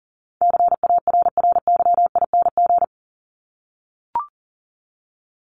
Morse code, heard as a keyed tone, about 40 wpm, 700 Hz. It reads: CARRYING
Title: ING Suffix - Top 100 Words with Base Words 40wpm